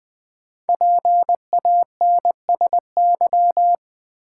Morse code, heard as a keyed tone, about 20 wpm, 690 Hz. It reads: PANSY
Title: Difficult Words with Hints - 20wpm